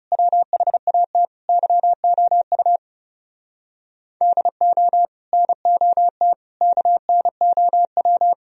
Morse code, heard as a keyed tone, about 37 wpm, 700 Hz. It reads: WHAT YOU DO NOT KNOW
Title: Speed-Racing - Sets of 5 Words 30wpm